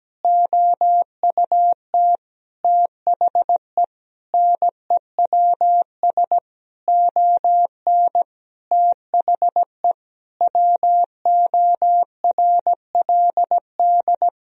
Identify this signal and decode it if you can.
Morse code, heard as a keyed tone, about 17 wpm, 700 Hz. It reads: OUT THE NEWS ON THE WORLD
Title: Sentences from Top 1000 Words - Review Part 4 - 17wpm